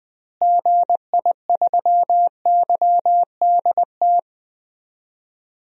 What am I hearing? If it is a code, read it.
Morse code, heard as a keyed tone, about 21 wpm, 700 Hz. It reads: GI3YDT
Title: DXCC-100 Call Signs - 6 Characters - Repeated 2x - 20wpm